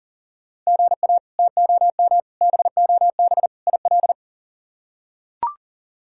Morse code, heard as a keyed tone, about 38 wpm, 700 Hz. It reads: GA TOM BOB IL
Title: K1USN SST QSO Practice - 40wpm